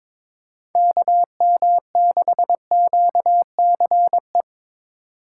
Morse code, heard as a keyed tone, about 26 wpm, 700 Hz. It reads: KM6QCE
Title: Speed Racing - USA 2x3 Call Signs 22wpm